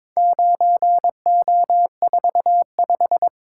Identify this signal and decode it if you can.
Morse code, heard as a keyed tone, about 22 wpm, 700 Hz. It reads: 9O45
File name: practice / 22wpm / Four Letter-Number Combinations - 22wpm